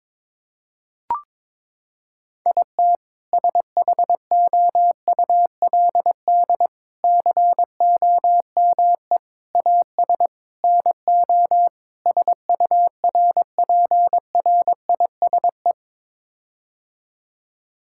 Morse code, heard as a keyed tone, about 22 wpm, 700 Hz. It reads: IT SHOULD COME AS NO SURPRISE
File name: practice / 22wpm / Sets of 6 Words - 22wpm